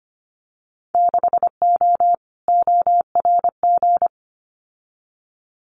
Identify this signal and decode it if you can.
Morse code, heard as a keyed tone, about 25 wpm, 700 Hz. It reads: 6O ORG
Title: Winter Field Day Exchanges Encore 25wpm